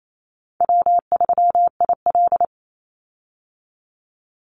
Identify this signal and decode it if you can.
Morse code, heard as a keyed tone, about 28 wpm, 700 Hz. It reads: W3IL
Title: Easy Call Signs - 28wpm